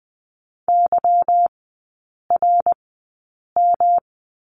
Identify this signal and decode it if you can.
Morse code, heard as a keyed tone, about 20 wpm, spaced as for 8 wpm, 690 Hz. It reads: YRM
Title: 159 Lesson 26e - Letter g - Three Characters 20wpm